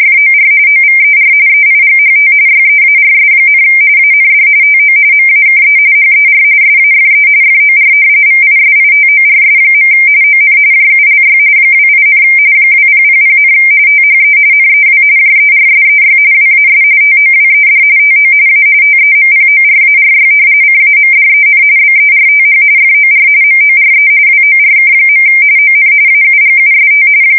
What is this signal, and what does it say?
RTTY, about 45.45 baud. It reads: FOR OVER EIGHT YEARS, AND TO MAKE HER SAFE FOR VISITORS. ON JULY 28, 2000, THE NEW JERSEY WAS TOWED TO A TEMPORARY CAMDEN BERTH FOR MORE WORK. ON AUGUST
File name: fsk170 / battleshipnj